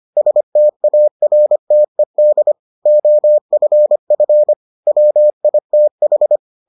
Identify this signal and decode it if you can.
Morse code, heard as a keyed tone, about 25 wpm, 600 Hz. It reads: STARTED OFF WITH